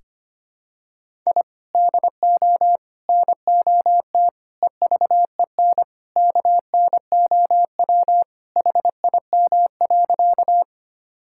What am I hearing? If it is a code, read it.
Morse code, heard as a keyed tone, about 25 wpm, 700 Hz. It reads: I DO NOT EVEN KNOW HIM.
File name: practice / 25wpm / Sentences with 4-Word Phrases - 25wpm